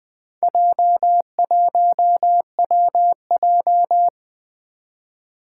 Morse code, heard as a keyed tone, about 21 wpm, 700 Hz. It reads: J1WJ
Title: Four Challenging Letter-Number Combinations 20wpm